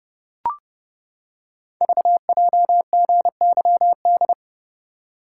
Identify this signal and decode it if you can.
Morse code, heard as a keyed tone, about 30 wpm, 700 Hz. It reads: VJGYD
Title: Code Groups - 5 Letters - 30wpm